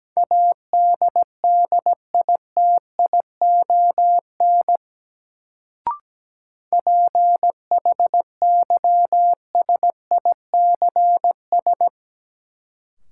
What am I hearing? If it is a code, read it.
Morse code, heard as a keyed tone, about 17 wpm, 700 Hz. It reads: ADDITION PHYSICS
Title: Sentences from Top 1500 Words - New Word Drill Alt - 17wpm